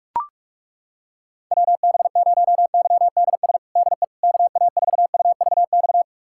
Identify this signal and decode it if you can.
Morse code, heard as a keyed tone, about 41 wpm, 700 Hz. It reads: WB0YDS DE KA4UVX
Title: USA Callsign DE Callsign - 6 Characters - 45wpm